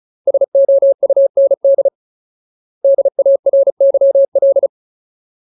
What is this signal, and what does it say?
Morse code, heard as a keyed tone, about 35 wpm, 550 Hz. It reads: SOUND DARYL
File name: Ultimate Word List / 35WPM Ultimate